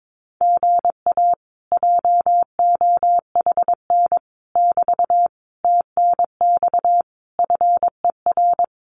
Morse code, heard as a keyed tone, about 22 wpm, 700 Hz. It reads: GA JOHN <BT> TNX FER
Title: Common QSO Phrases Encore 22wpm